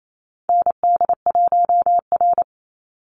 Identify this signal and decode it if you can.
Morse code, heard as a keyed tone, about 28 wpm, 700 Hz. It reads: ND1R
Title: Call Signs - 28wpm